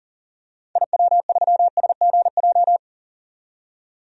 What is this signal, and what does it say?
Morse code, heard as a keyed tone, about 40 wpm, 700 Hz. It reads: IW3SGJ
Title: DXCC-100 Call Signs - 6 Characters - 40wpm